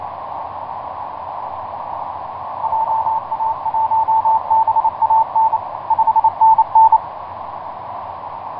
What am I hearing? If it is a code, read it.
Morse code, heard as a keyed tone, about 28 wpm, 890 Hz. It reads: JA1BAN 5NN
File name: vk9x-g6ay iv3kvc etc 12m cw 23feb2010 0803